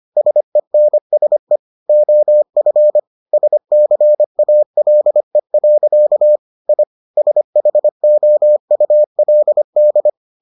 Morse code, heard as a keyed tone, about 25 wpm, 600 Hz. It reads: SENSE OF SCALE. I SHOULD